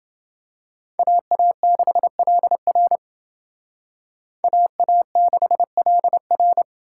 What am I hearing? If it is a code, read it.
Morse code, heard as a keyed tone, about 28 wpm, 700 Hz. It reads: AA6LR AA6LR